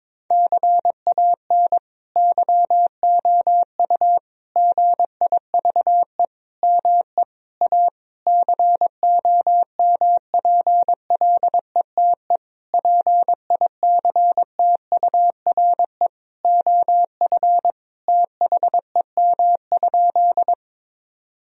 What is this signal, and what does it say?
Morse code, heard as a keyed tone, about 22 wpm, 700 Hz. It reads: CAN YOU GIVE ME A COMPLETE PICTURE OF THEM?
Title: Sentences from Top 1000 Words - Review Part 2 - 22wpm